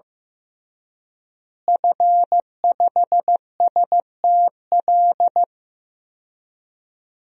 Morse code, heard as a keyed tone, about 15 wpm, 700 Hz. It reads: F5STL
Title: DXCC-100 Call Signs - 5 Characters - Repeated 3x - 15wpm